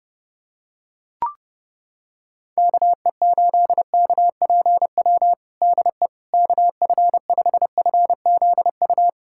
Morse code, heard as a keyed tone, about 32 wpm, 700 Hz. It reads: KE8KPW DE KF5FZU
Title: USA Callsign DE Callsign 30wpm